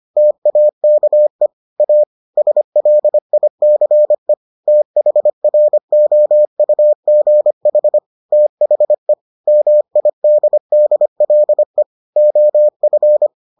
Morse code, heard as a keyed tone, about 25 wpm, 600 Hz. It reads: TAKE A SLICE THROUGH THE MIDDLE OF